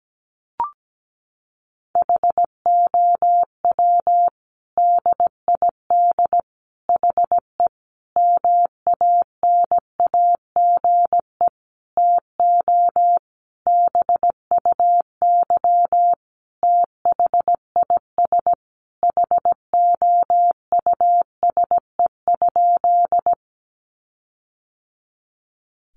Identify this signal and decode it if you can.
Morse code, heard as a keyed tone, about 17 wpm, 700 Hz. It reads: HOW DID HE MANAGE TO BUY THIS HOUSE?
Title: Sentences from Top 1400 Words 17wpm